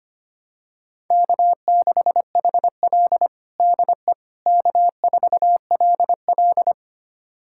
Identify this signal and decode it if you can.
Morse code, heard as a keyed tone, about 25 wpm, 700 Hz. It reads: K6HL DE K4LL
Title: USA Callsign DE Callsign - 4 Characters - 25wpm